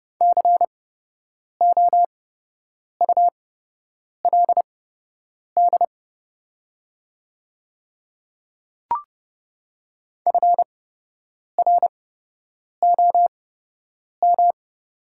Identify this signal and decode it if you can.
Morse code, heard as a keyed tone, about 30 wpm, spaced as for 8 wpm, 700 Hz. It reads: COULD FROM
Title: 8x Character Spacing - Top 100 Words - 30wpm